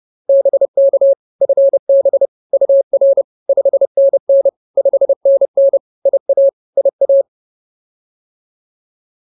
Morse code, heard as a keyed tone, about 30 wpm, 550 Hz. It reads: BK FB UR 5NN 5NN IA IA